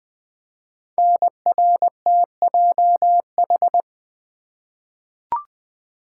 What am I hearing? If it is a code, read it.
Morse code, heard as a keyed tone, about 20 wpm, 700 Hz. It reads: NRTJH
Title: Code Groups - 5 Letters - 20wpm